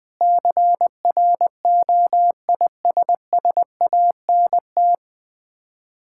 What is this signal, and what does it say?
Morse code, heard as a keyed tone, about 19 wpm, 700 Hz. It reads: CROISSANT
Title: Difficult Words with Hints - 20wpm